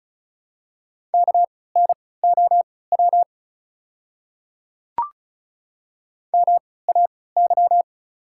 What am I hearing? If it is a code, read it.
Morse code, heard as a keyed tone, about 36 wpm, spaced as for 20 wpm, 700 Hz. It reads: KNOW MAY
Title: 3x Character Spacing - Top 200 Words - 35wpm